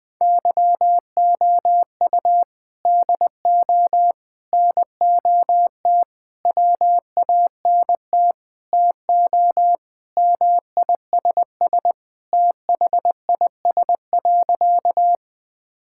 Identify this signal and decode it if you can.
Morse code, heard as a keyed tone, about 19 wpm, 700 Hz. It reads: YOU DO NOT WANT TO MISS THIS.
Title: Sentences with 4-Word Phrases - Spoken Beginning - 20wpm